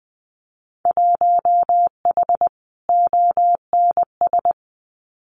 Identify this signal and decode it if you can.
Morse code, heard as a keyed tone, about 20 wpm, 700 Hz. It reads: 1H ONS
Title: Winter Field Day Exchanges 20wpm